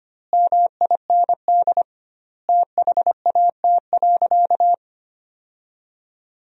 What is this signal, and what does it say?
Morse code, heard as a keyed tone, about 25 wpm, 700 Hz. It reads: MIND THAT.
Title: 2x Word-Spacing - Sentences from Top 500 Words - 25wpm